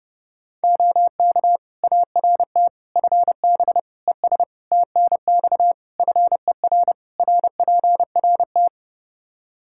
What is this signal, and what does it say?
Morse code, heard as a keyed tone, about 32 wpm, 700 Hz. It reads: OK ART FB ES TNX FER RPRT
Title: Practice QSO - Exchange 2 - Info - 30wpm